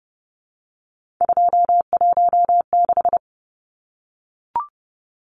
Morse code, heard as a keyed tone, about 30 wpm, 700 Hz. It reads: 216
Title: Numbers - 3-Digit 30wpm